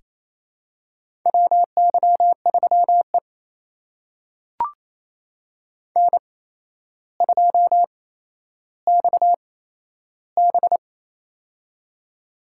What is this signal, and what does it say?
Morse code, heard as a keyed tone, about 27 wpm, spaced as for 12 wpm, 700 Hz. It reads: WY3E N2XB
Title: USA Call Signs - 4 Characters - 8x Character Spacing - 28wpm